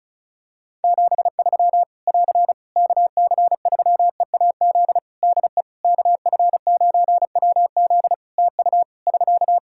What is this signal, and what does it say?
Morse code, heard as a keyed tone, about 34 wpm, 700 Hz. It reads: 73 <AR> KC3EAZ DE KF9WZ TU <SK>
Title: Practice QSO - Exchange 4 - Ending - 35wpm